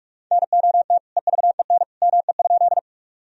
Morse code, heard as a keyed tone, about 42 wpm, 700 Hz. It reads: NOT EVEN ME?
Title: Sentences from Top 1000 Words - Review Part 1 - 45wpm